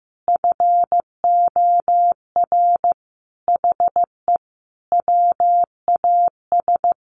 Morse code, heard as a keyed tone, about 14 wpm, 700 Hz. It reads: FOR HE WAS
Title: Sentences from Top 1500 Words 15wpm